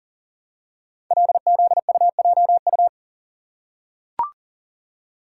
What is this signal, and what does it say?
Morse code, heard as a keyed tone, about 38 wpm, 700 Hz. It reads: LZUJU
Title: Code Groups - 5 Letters - 40wpm